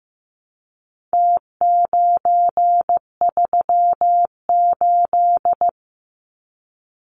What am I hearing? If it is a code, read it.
Morse code, heard as a keyed tone, about 15 wpm, 700 Hz. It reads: T938
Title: Numbers - Contest Style - 4-Digit 15wpm